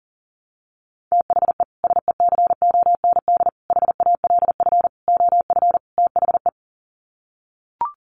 Morse code, heard as a keyed tone, about 40 wpm, 700 Hz. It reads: THE SECOND HALF OF THE
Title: Sets of 5 Words 40wpm